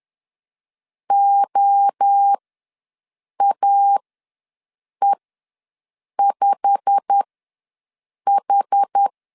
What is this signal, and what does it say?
Morse code, heard as a keyed tone, about 10 wpm, spaced as for 6 wpm, 800 Hz. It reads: OAE5H
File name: LEZIONI / 8 YJ45 / 8 YJ45 ws 10wpm 3843